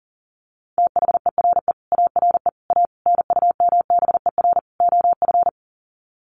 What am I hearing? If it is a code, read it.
Morse code, heard as a keyed tone, about 40 wpm, 700 Hz. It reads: THERE ARE A NUMBER OF
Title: Sets of 5 Words 40wpm